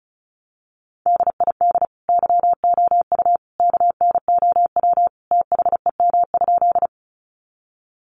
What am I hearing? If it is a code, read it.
Morse code, heard as a keyed tone, about 34 wpm, 700 Hz. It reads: DID YOU KNOW THEM?
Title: Sentences from Top 100 Words 35wpm